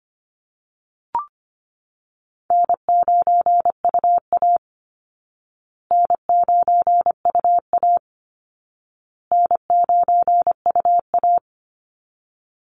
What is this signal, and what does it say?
Morse code, heard as a keyed tone, about 25 wpm, 700 Hz. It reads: N9UA N9UA N9UA
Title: Easy Call Signs Repeated 3x - 25wpm